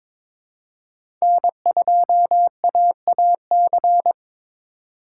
Morse code, heard as a keyed tone, about 22 wpm, 700 Hz. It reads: N2AAC
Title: USA Call Signs - 5 Characters - 3x Character Spacing - 22wpm